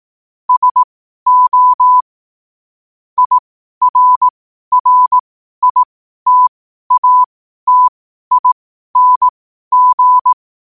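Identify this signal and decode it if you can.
Morse code, heard as a keyed tone, about 18 wpm, spaced as for 12 wpm, 1000 Hz. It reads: SO IRRITATING
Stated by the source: Morse code quote-of-the-day practice at 12 words per minute.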